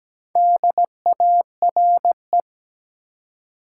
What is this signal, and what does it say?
Morse code, heard as a keyed tone, about 17 wpm, 700 Hz. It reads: DARE
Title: Four Letter Words - 17wpm